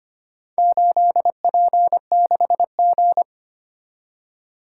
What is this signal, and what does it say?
Morse code, heard as a keyed tone, about 25 wpm, 700 Hz. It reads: 8P6G
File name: practice / 25wpm / Four Letter-Number Combinations - 25wpm